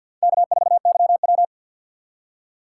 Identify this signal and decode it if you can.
Morse code, heard as a keyed tone, about 48 wpm, 700 Hz. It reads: K4YW
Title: USA Call Signs - 4 Characters - 3x Character Spacing - 50wpm